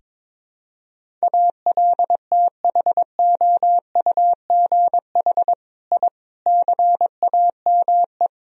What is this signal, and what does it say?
Morse code, heard as a keyed tone, about 22 wpm, 700 Hz. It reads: ALTHOUGH I CAME
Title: Sentences from Top 1000 Words - Review Part 3 - 22wpm